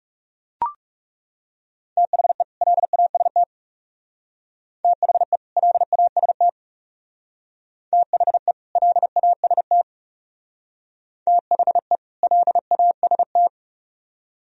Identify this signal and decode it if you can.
Morse code, heard as a keyed tone, about 35 wpm, 700 Hz. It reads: THE LAST THE LAST THE LAST THE LAST
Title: Speed-Racing - Sets of 2 Words 30wpm